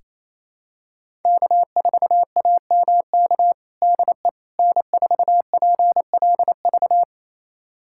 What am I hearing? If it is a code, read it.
Morse code, heard as a keyed tone, about 27 wpm, 700 Hz. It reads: K4AMK DE N4PLV
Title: USA Callsign DE Callsign - 5 Characters - 28wpm